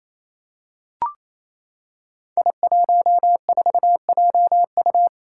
Morse code, heard as a keyed tone, about 27 wpm, 700 Hz. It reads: I14JU
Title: Five Challenging Letter-Number Combinations 28wpm